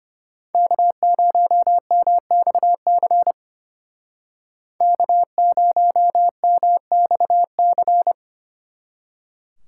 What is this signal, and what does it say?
Morse code, heard as a keyed tone, about 33 wpm, 700 Hz. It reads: K0MXC K0MXC